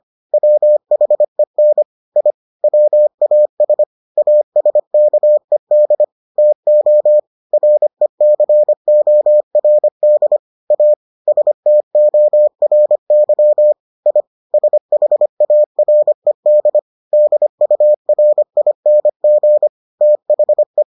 Morse code, heard as a keyed tone, about 25 wpm, 600 Hz. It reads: WHEN I WAS ASKED TO RECORD A STORY I SHARED DURING THE